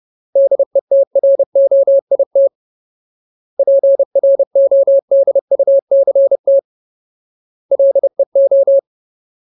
Morse code, heard as a keyed tone, about 30 wpm, 550 Hz. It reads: DETROIT PRODUCT LEO
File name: Ultimate Word List / 30WPM Ultimate